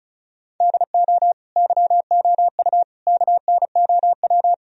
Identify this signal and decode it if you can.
Morse code, heard as a keyed tone, about 36 wpm, 700 Hz. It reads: DO YOU KNOW
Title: Sentences from Top 1000 Words - Review Part 4 - 35wpm